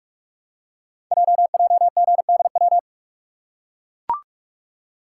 Morse code, heard as a keyed tone, about 44 wpm, 690 Hz. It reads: JJGDW
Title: Code Groups - 5 Letters - 45wpm